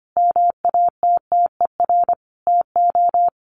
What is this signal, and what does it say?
Morse code, heard as a keyed tone, about 25 wpm, 700 Hz. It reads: MATTER TO
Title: Sentences from Top 700 Words 25wpm